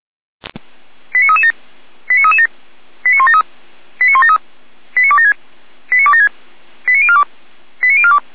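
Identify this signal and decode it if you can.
Five-tone selective call (ZVEI), heard as quick runs of five tones, tones of about 70 ms. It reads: 89337 89337 89173 89173 89276 89276 89042 89042
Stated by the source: ZVEI five-tone selective call received by SDR.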